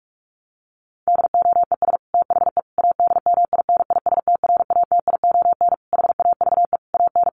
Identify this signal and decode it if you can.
Morse code, heard as a keyed tone, about 44 wpm, 700 Hz. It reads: DOES THE ADMINISTRATION HAVE AN
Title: Sentences from Top 1300 Words 45wpm